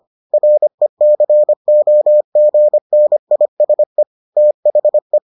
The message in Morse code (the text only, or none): RECOGNISE THE